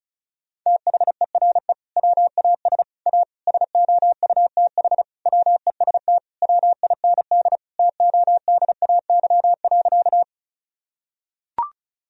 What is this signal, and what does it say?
Morse code, heard as a keyed tone, about 36 wpm, 700 Hz. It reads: THERE WAS A SOUTH WEST WIND TODAY.
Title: Sentences from Top 1000 Words - Review Part 2 - 35wpm